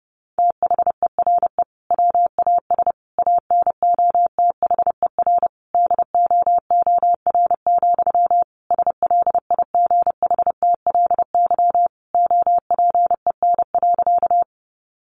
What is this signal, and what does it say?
Morse code, heard as a keyed tone, about 30 wpm, 700 Hz. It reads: THERE WAS ANOTHER DOOR, SLIGHTLY OPEN.
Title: Sentences from Top 1300 Words 30wpm